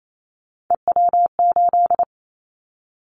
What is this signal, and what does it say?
Morse code, heard as a keyed tone, about 28 wpm, 700 Hz. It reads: EW8
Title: Callsign Prefixes for Top 50 DXCC - 28wpm